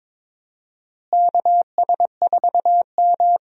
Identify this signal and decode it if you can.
Morse code, heard as a keyed tone, about 21 wpm, 700 Hz. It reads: KS4M
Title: USA Call Signs - 4 Characters - Repeated 3x - 22wpm